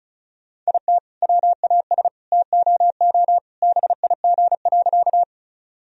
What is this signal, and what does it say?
Morse code, heard as a keyed tone, about 36 wpm, 700 Hz. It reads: IT WAS TOO BIG.
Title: Sentences from Top 1000 Words - Review Part 1 - 35wpm